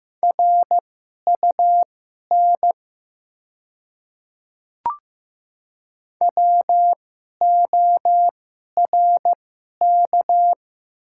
Morse code, heard as a keyed tone, about 15 wpm, spaced as for 11 wpm, 700 Hz. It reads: RUN WORK
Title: 2x Character Spacing - Top 300 Words - 15wpm